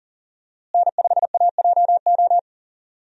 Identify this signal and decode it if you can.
Morse code, heard as a keyed tone, about 39 wpm, 700 Hz. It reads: N5AJO
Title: USA Call Signs - 5 Characters - Repeated 3x - 40wpm